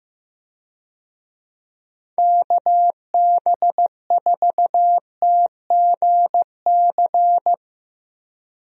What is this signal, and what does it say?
Morse code, heard as a keyed tone, about 15 wpm, 700 Hz. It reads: KB4TGC
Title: USA Call Signs - 6 Characters - Repeated 2x - 15wpm